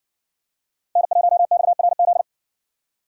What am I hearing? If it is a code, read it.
Morse code, heard as a keyed tone, about 60 wpm, 700 Hz. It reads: N1XRZ